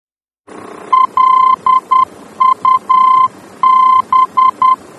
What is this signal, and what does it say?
Morse code, heard as a keyed tone, about 10 wpm, 1040 Hz. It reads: LUB
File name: germany / mx / LUB-D-VOR